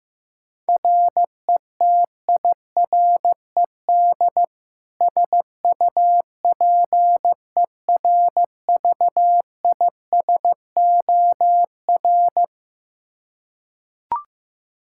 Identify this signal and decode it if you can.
Morse code, heard as a keyed tone, about 15 wpm, 700 Hz. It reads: RETIRED SUPERVISOR
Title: Practice QSO - Exchange 3 - More Info - 15wpm